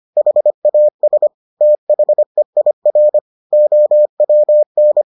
Morse code, heard as a keyed tone, about 25 wpm, 600 Hz. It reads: HAS THEIR OWN